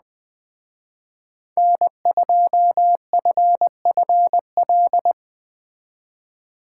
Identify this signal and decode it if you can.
Morse code, heard as a keyed tone, about 20 wpm, 700 Hz. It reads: N2FFL